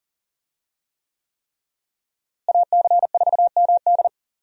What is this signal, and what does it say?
Morse code, heard as a keyed tone, about 39 wpm, 700 Hz. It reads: AC4MD
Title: USA Call Signs - 5 Characters - Repeated 3x - 40wpm